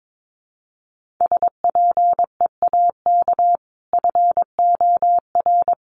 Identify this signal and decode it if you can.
Morse code, heard as a keyed tone, about 22 wpm, 700 Hz. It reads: SPEAK FOR
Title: Sentences from Top 1100 Words 22wpm